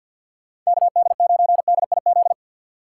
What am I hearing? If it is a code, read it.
Morse code, heard as a keyed tone, about 63 wpm, 700 Hz. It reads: KD9DIZ